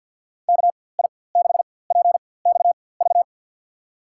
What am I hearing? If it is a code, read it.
Morse code, heard as a keyed tone, about 52 wpm, spaced as for 23 wpm, 700 Hz. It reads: KI6PXV
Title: USA Call Signs - 6 Characters - 4x Character Spacing - 50wpm